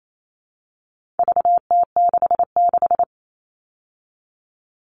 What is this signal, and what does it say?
Morse code, heard as a keyed tone, about 28 wpm, 700 Hz. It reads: VT66
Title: All Cut Numbers - 4-Digit 28wpm